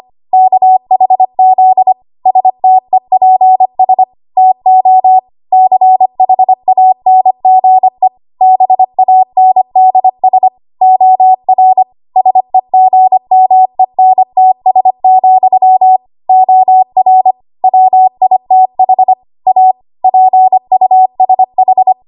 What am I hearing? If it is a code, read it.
Morse code, heard as a keyed tone, about 25 wpm, 750 Hz. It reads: KHZ STEPS TO CHANGE BANDS OR SEGMENTS, OR WITH A PUSH
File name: Archive / 25 WPM / 25WPM